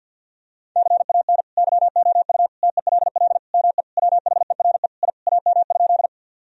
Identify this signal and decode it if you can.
Morse code, heard as a keyed tone, about 48 wpm, 700 Hz. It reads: CAN YOU TELL ME WHERE I AM?
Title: Sentences from Top 1000 Words - Review Part 2 - 50wpm